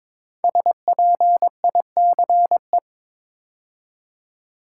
Morse code, heard as a keyed tone, about 23 wpm, 700 Hz. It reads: SPICE
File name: 5 Letter Words - First 3 Letters Given - 22wpm